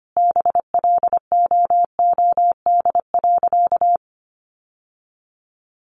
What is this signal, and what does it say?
Morse code, heard as a keyed tone, about 25 wpm, 700 Hz. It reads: BLOOD.
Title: Sentences from Top 1100 Words 25wpm